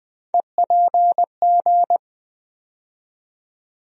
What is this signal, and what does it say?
Morse code, heard as a keyed tone, about 20 wpm, 700 Hz. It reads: EPG
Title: USA Call Sign Postfixes - 3 Letters 20wpm